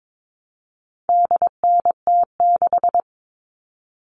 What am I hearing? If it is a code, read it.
Morse code, heard as a keyed tone, about 22 wpm, 700 Hz. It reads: DNT6
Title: All Cut Numbers - 4-Digit 22wpm